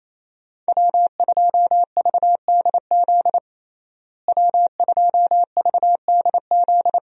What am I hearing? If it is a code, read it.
Morse code, heard as a keyed tone, about 28 wpm, 700 Hz. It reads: W2VDZ W2VDZ
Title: USA Call Signs - 5 Characters - Repeated 2x - 28wpm